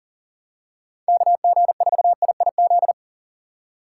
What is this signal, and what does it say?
Morse code, heard as a keyed tone, about 40 wpm, 700 Hz. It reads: KG4IIZ
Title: USA Call Signs - 6 Characters - 2x Character Spacing - 40wpm